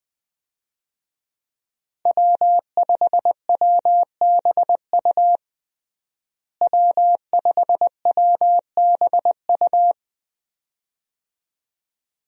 Morse code, heard as a keyed tone, about 20 wpm, 700 Hz. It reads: W5WBU W5WBU
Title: USA Call Signs - 5 Characters - Repeated 2x - 20wpm